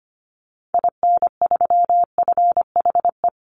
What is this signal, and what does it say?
Morse code, heard as a keyed tone, about 25 wpm, 700 Hz. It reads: IN3FHE
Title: Call Signs Repeated 2x - 25wpm